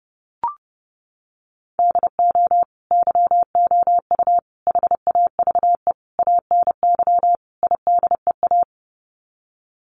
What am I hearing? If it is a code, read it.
Morse code, heard as a keyed tone, about 30 wpm, 700 Hz. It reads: DO YOU HAVE ANY IDEA
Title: Sets of 5 Words Encore 30wpm